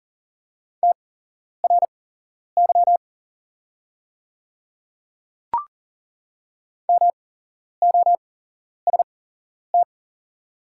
Morse code, heard as a keyed tone, about 40 wpm, spaced as for 11 wpm, 690 Hz. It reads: TRY MOST
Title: 8x Character Spacing - Top 200 Words - 40wpm